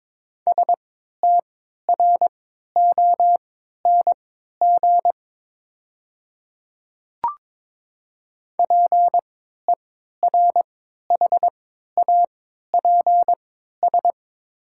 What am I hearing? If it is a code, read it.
Morse code, heard as a keyed tone, about 22 wpm, spaced as for 13 wpm, 700 Hz. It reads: STRONG PERHAPS
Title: 3x Character Spacing - Top 500 Words - 22wpm